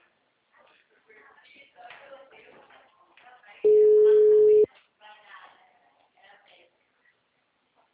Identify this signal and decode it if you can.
Ringback tone (Europe, 425 Hz), with speech.
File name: out- -2021- - - .0